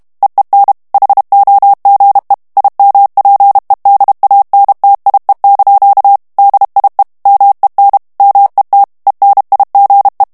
Morse code, heard as a keyed tone, about 32 wpm, 800 Hz. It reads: EEN HOGE IMPEDANTIE) DIE MEN MET ENIGE
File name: HOLLANDAIS texte clair 1 vitesse 30 mots minute